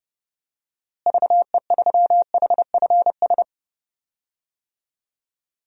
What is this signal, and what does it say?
Morse code, heard as a keyed tone, about 31 wpm, 700 Hz. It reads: VE3HFS
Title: DXCC-100 Call Signs - 6 Characters - Repeated 3x - 30wpm